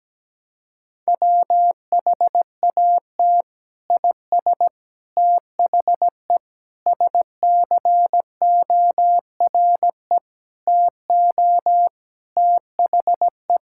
Morse code, heard as a keyed tone, about 17 wpm, 700 Hz. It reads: WHAT IS THE SCORE TO THE
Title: Sentences from Top 1600 Words 17wpm